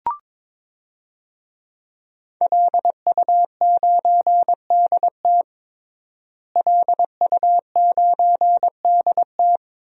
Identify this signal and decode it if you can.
Morse code, heard as a keyed tone, about 21 wpm, 700 Hz. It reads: LU9DT LU9DT
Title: DXCC-100 Call Signs - 5 Characters - Repeated 2x - 22wpm